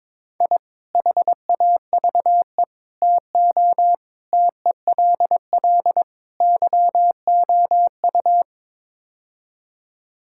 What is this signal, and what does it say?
Morse code, heard as a keyed tone, about 22 wpm, 700 Hz. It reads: I HAVE TO TELL YOU
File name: practice / 22wpm / Build A Phrase - 5-Word Phrases - 22wpm